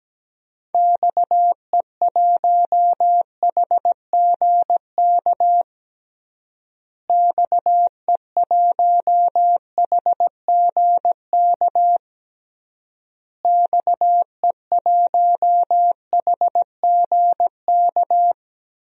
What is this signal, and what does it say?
Morse code, heard as a keyed tone, about 17 wpm, 700 Hz. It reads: XE1HGK XE1HGK XE1HGK
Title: DXCC-100 Call Signs - 6 Characters - Repeated 3x - 17wpm